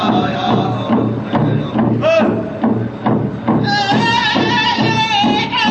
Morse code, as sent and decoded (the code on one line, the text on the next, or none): none